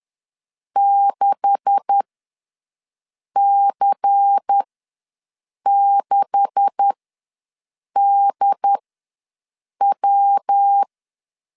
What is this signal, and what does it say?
Morse code, heard as a keyed tone, about 10 wpm, spaced as for 6 wpm, 800 Hz. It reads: 6C6DW
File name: LEZIONI / 9 286 ws 10wpm 4032